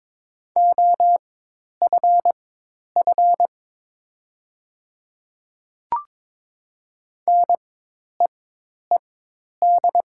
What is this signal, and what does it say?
Morse code, heard as a keyed tone, about 23 wpm, spaced as for 10 wpm, 700 Hz. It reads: OFF NEED
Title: 4x Character Spacing - Top 500 Words - 22wpm